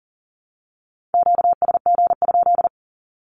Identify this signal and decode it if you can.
Morse code, heard as a keyed tone, about 40 wpm, 700 Hz. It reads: QSG?
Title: Q-Codes 40wpm